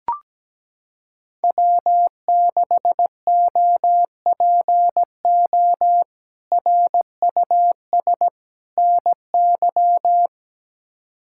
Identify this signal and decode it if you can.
Morse code, heard as a keyed tone, about 17 wpm, 700 Hz. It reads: W6OPO RUS NY
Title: K1USN SST QSO Practice - 17wpm